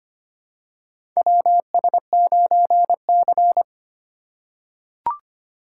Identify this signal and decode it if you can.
Morse code, heard as a keyed tone, about 25 wpm, 690 Hz. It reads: WS9C
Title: USA Call Signs - 4 Characters - 4x Character Spacing - 25wpm